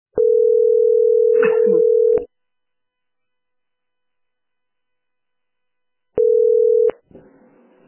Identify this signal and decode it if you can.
Ringback tone (North America), 440 + 480 Hz.